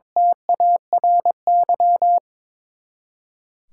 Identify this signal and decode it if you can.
Morse code, heard as a keyed tone, about 22 wpm, 700 Hz. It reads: TARY
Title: K1USN SST QSO Practice 22wpm